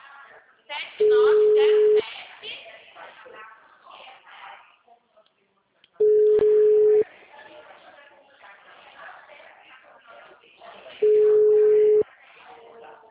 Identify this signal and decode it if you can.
Ringback tone (Europe, 425 Hz), with speech.
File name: out- -2027- - - .37130